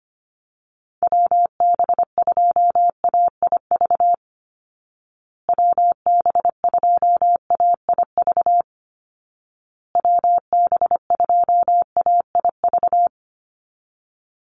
Morse code, heard as a keyed tone, about 25 wpm, 700 Hz. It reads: WB2AIV WB2AIV WB2AIV
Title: Call Signs Repeated 2x - 25wpm